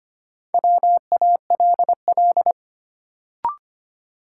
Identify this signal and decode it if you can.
Morse code, heard as a keyed tone, about 25 wpm, 700 Hz. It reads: WALL
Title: Sentences from Top 1100 Words - New Word Drill Alt - 25wpm